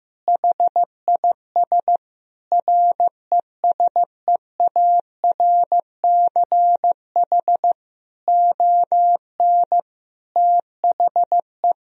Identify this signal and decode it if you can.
Morse code, heard as a keyed tone, about 15 wpm, 700 Hz. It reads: HIS RESEARCH ON THE
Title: Sentences from Top 1000 Words - Review Part 4 - 15wpm